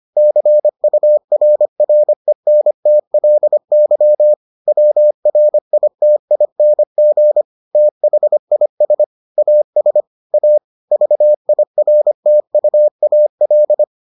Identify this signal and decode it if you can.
Morse code, heard as a keyed tone, about 25 wpm, 600 Hz. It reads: CURRENTLY WRITING THIS AS A VIRTUAL